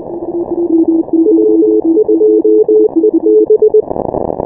Morse code, heard as a keyed tone, about 20 wpm, 430 Hz. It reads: W2EB
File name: final / W1UJ-168-140-MorseRunner